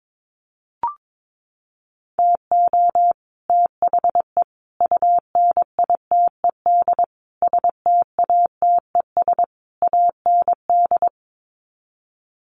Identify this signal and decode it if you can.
Morse code, heard as a keyed tone, about 22 wpm, 700 Hz. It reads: TO THE UNITED STATES AND
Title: Sets of 5 Words Encore 22wpm